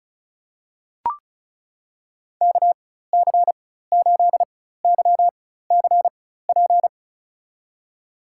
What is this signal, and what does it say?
Morse code, heard as a keyed tone, about 36 wpm, spaced as for 16 wpm, 700 Hz. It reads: KC8YCP
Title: USA Call Signs - 4x Character Spacing - 35wpm